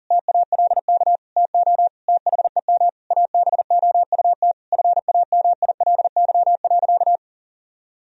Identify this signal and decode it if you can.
Morse code, heard as a keyed tone, about 40 wpm, 700 Hz. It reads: TALK TO THEM ABOUT FAMILY.
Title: Sentences from Top 1000 Words - Review Part 3 - 40wpm